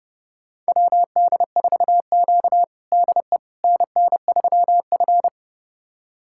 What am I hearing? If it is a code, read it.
Morse code, heard as a keyed tone, about 32 wpm, 700 Hz. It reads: WD4Q DE NN3F
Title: USA Callsign DE Callsign - 4 Characters - 30wpm